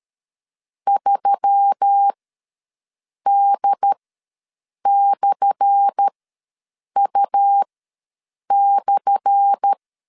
Morse code, heard as a keyed tone, about 12 wpm, spaced as for 7 wpm, 800 Hz. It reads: 3D/U/
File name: LEZIONI / 10 SIGNS / 10C DOT ws 12wpm 3548